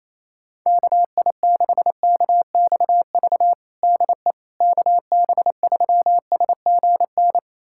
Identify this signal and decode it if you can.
Morse code, heard as a keyed tone, about 28 wpm, 700 Hz. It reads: KI6KXV DE KB3SGN
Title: USA Callsign DE Callsign 28wpm